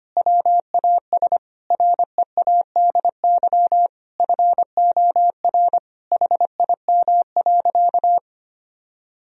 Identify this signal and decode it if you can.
Morse code, heard as a keyed tone, about 25 wpm, 700 Hz. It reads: WAS READY FOR HIM.
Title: Sentences from Top 1000 Words - Review Part 3 - 25wpm